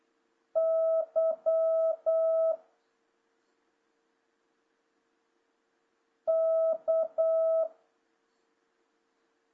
Morse code, heard as a keyed tone, about 8 wpm, 650 Hz. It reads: Y K
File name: jh8cbh / 2026cw